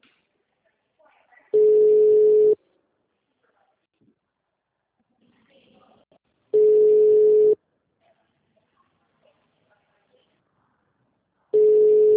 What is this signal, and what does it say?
Ringback tone (Europe, 425 Hz), with speech.